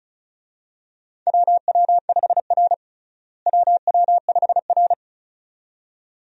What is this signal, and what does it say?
Morse code, heard as a keyed tone, about 36 wpm, 700 Hz. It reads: WW5R WW5R
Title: USA Call Signs - 4 Characters - Repeated 2x - 35wpm